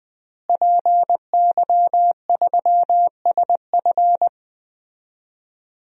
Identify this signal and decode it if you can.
Morse code, heard as a keyed tone, about 20 wpm, 700 Hz. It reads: PY3SF
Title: DXCC-100 Call Signs - 5 Characters - Repeated 2x - 20wpm